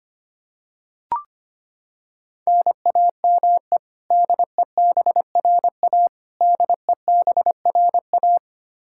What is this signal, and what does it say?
Morse code, heard as a keyed tone, about 25 wpm, 700 Hz. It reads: NAME DEBRA DEBRA
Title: Practice QSO - Exchange 1 - Beginning - 25wpm